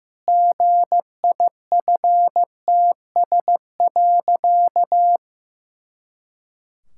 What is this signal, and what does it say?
Morse code, heard as a keyed tone, about 16 wpm, 700 Hz. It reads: GIFTS.
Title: Pangram Challenge v2 - 15wpm